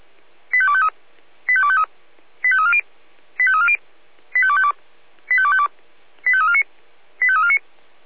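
Five-tone selective call (ZVEI), heard as quick runs of five tones, tones of about 70 ms. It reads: 85253 85253 85340 85340 85252 85252 85349 85349
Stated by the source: ZVEI five-tone call received by SDR.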